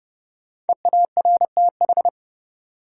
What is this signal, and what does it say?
Morse code, heard as a keyed tone, about 32 wpm, 700 Hz. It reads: EARTH
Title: Sentences from Top 200 Words - New Word Drill - 30wpm